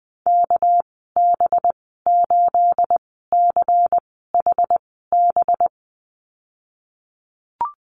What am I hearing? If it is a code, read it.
Morse code, heard as a keyed tone, about 20 wpm, spaced as for 14 wpm, 700 Hz. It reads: KB8CHB
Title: 213 Lesson 34g - Number 8 - Call Signs 20wpm